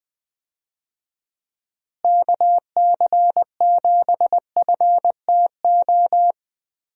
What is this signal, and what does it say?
Morse code, heard as a keyed tone, about 20 wpm, 700 Hz. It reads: KC7FTO
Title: USA Call Signs - 6 Characters - Repeated 2x - 20wpm